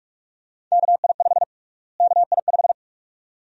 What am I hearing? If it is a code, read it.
Morse code, heard as a keyed tone, about 41 wpm, 700 Hz. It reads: KI5 KI5
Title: USA Call Sign Prefixes - Repeated 2x - 45wpm